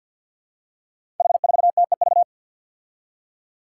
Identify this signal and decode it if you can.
Morse code, heard as a keyed tone, about 51 wpm, 700 Hz. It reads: H4TEV
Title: Five Challenging Letter-Number Combinations 50wpm